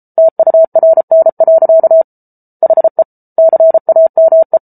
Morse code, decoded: TURN. HE CAME

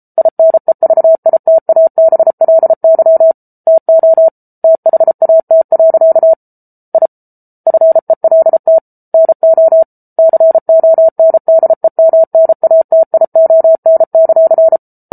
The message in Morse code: INEVITABLY TO THAT. I FELT NO CONDEMNATION;

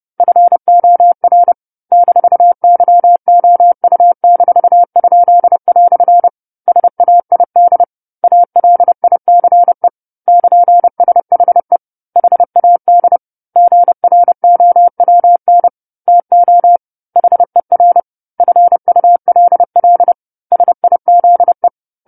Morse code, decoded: FOR -YOU-?" SAID ALICE <KN>SHE HAD GROWN TO HER FULL SIZE

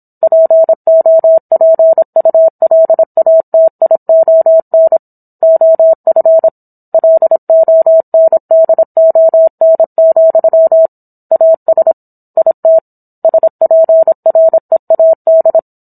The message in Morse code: POPULATION OF LONDON, AS IT SPREAD

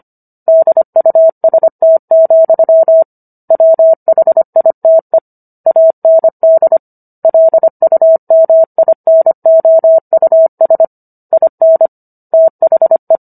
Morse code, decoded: DUST, WHITE AND LUMINOUS IN THE

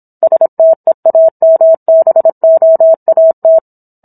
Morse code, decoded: STEAMBOAT

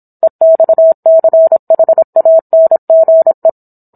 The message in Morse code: EXCHANGE

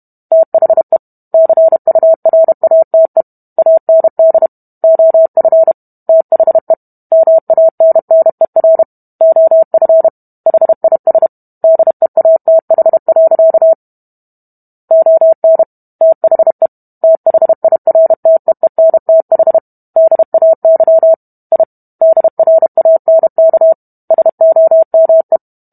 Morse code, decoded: THE CURATE AND OF THE MANNER OF HIS DEATH. ON THE THIRTEENTH DAY I DRANK SOME

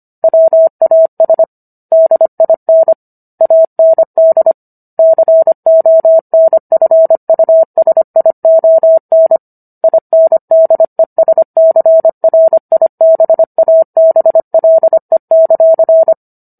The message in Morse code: WAS DIN AND CONFUSION INDESCRIBABLE;

